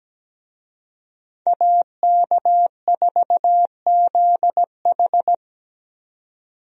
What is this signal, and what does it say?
Morse code, heard as a keyed tone, about 17 wpm, 700 Hz. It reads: AK4ZH
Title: USA Call Signs - 5 Characters - 17wpm